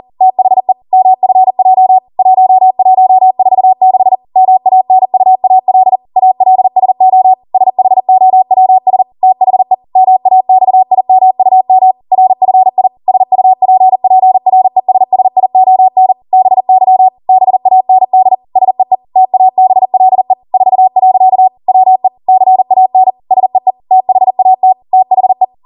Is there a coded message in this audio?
Yes: Morse code from 0.2 to 25.4 s, heard as a keyed tone, about 40 wpm, 750 Hz. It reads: THE MFJ 1146 MANUAL ALSO SHOWS THE MAXIMUM RFI SUPPRESSION BY BAND SEE TABLE 4. WE CAN SEE THAT THE